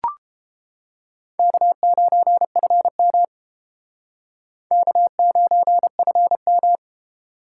Morse code, heard as a keyed tone, about 29 wpm, 700 Hz. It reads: K9FM K9FM